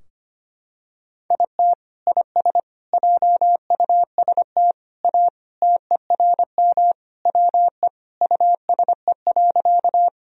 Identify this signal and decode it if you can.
Morse code, heard as a keyed tone, about 25 wpm, 700 Hz. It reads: IT IS JUST A TERM WE USE.
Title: Sentences from Top 1000 Words - Review Part 4 - 25wpm